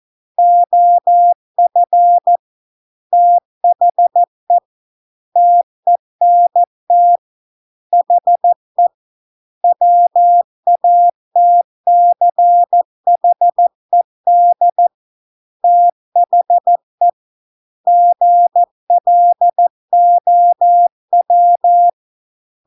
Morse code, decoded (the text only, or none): OF THE TENT HE WATCHED THE GLOW